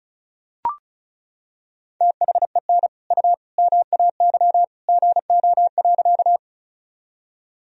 Morse code, heard as a keyed tone, about 34 wpm, 700 Hz. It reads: THEN U MAY GO.
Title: Sentences from Top 100 Words and Abbreviations 35wpm